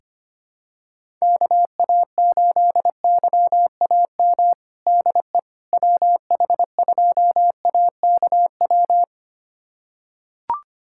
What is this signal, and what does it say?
Morse code, heard as a keyed tone, about 25 wpm, 700 Hz. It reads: KA8YAM DE WH2AKW
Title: USA Callsign DE Callsign - 6 Characters - 25wpm